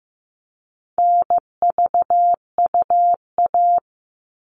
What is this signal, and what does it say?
Morse code, heard as a keyed tone, about 15 wpm, 700 Hz. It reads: NVUA
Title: All Cut Numbers - 4-Digit 15wpm